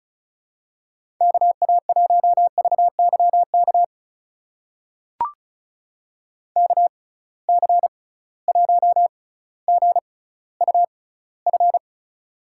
Morse code, heard as a keyed tone, about 36 wpm, spaced as for 18 wpm, 700 Hz. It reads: KA1VYK KC1GUF
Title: USA Call Signs - 6 Characters - 6x Character Spacing - 35wpm